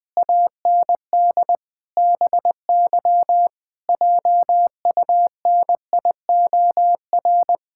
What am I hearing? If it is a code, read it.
Morse code, heard as a keyed tone, about 20 wpm, 700 Hz. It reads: AND BY JUNIOR
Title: Pangram Challenge v2 - 20wpm